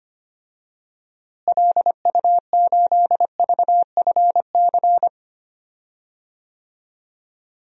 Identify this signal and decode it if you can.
Morse code, heard as a keyed tone, about 25 wpm, 700 Hz. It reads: LU8VFC
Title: DXCC-100 Call Signs - 6 Characters - Repeated 3x - 25wpm